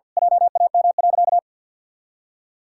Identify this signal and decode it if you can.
Morse code, heard as a keyed tone, about 47 wpm, 700 Hz. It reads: JAM.
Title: Pangram Challenge v2 - 50wpm